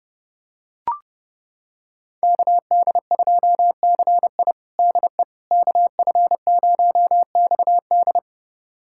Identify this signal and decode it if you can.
Morse code, heard as a keyed tone, about 30 wpm, 700 Hz. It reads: KD2CI DE KF0XD
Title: USA Callsign DE Callsign - 5 Characters - 30wpm